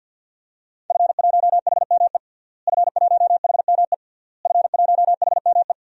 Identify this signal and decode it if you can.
Morse code, heard as a keyed tone, about 53 wpm, 700 Hz. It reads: F1HME F1HME F1HME
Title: DXCC-100 Call Signs - 5 Characters - Repeated 3x - 50wpm